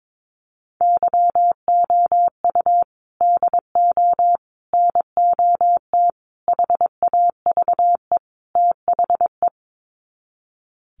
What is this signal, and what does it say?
Morse code, heard as a keyed tone, about 22 wpm, 700 Hz. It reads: YOU DO NOT HAVE THE
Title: Sets of 5 Words 22wpm